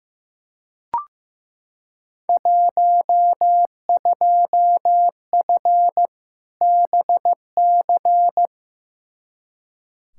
Morse code, heard as a keyed tone, about 15 wpm, 700 Hz. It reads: 12F BC
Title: Field Day Exchanges Encore - 15wpm